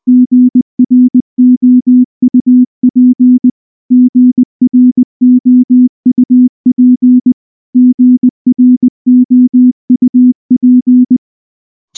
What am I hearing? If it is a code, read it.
Morse code, heard as a keyed tone, about 19 wpm, 250 Hz. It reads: GROUP GROUP GROUP
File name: morse / morse-training-20WPM